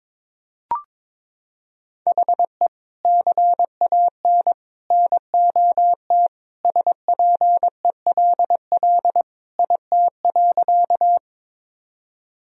Morse code, heard as a keyed tone, about 22 wpm, 700 Hz. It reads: HE CAN NOT SPELL IT.
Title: Sentences from Top 1000 Words - Review Part 1 - 22wpm